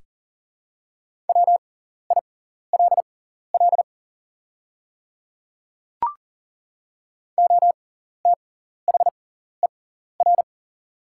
Morse code, heard as a keyed tone, about 40 wpm, spaced as for 14 wpm, 700 Hz. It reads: WILL OTHER
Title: 6x Character Spacing - Top 100 Words - 40wpm